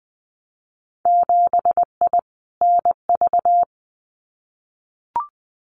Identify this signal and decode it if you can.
Morse code, heard as a keyed tone, about 20 wpm, 700 Hz. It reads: 7I NV
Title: Winter Field Day Exchanges Encore 20wpm